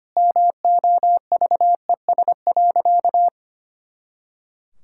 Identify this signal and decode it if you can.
Morse code, heard as a keyed tone, about 25 wpm, 700 Hz. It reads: MOVES.
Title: Pangram Challenge v2 - 25wpm